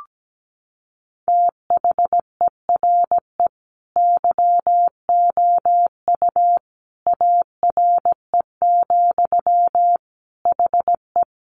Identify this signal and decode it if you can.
Morse code, heard as a keyed tone, about 17 wpm, 700 Hz. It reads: THERE YOU ARE, HE
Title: Sentences from Top 100 Words 17wpm